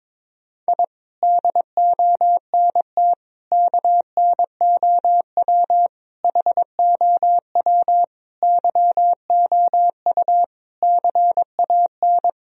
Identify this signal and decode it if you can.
Morse code, heard as a keyed tone, about 22 wpm, 700 Hz. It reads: I DONT KNOW HOW YOU CAN
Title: Sentences from Top 1000 Words - Review Part 2 - 22wpm